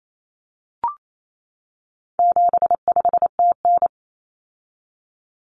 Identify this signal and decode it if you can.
Morse code, heard as a keyed tone, about 28 wpm, 700 Hz. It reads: 75TN
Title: Common Cut Numbers - 4-Digit 28wpm